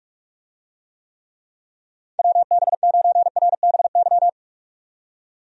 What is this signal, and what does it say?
Morse code, heard as a keyed tone, about 42 wpm, 700 Hz. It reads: WB9RBY
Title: USA Call Signs - 6 Characters - 45wpm